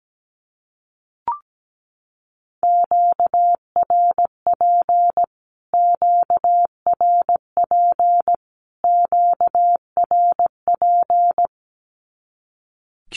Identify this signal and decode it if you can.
Morse code, heard as a keyed tone, about 17 wpm, 700 Hz. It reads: QRP QRP QRP
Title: Q-Codes Repeated 3x 17wpm